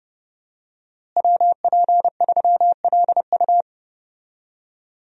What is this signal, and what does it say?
Morse code, heard as a keyed tone, about 30 wpm, 700 Hz. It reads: WP3LU
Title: USA Call Signs - 5 Characters - Repeated 2x - 30wpm